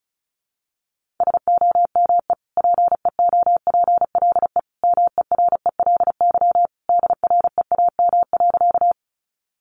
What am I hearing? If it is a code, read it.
Morse code, heard as a keyed tone, about 34 wpm, 700 Hz. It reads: SOME PEOPLE MERELY DREAM.
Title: Sentences from Top 1500 Words 35wpm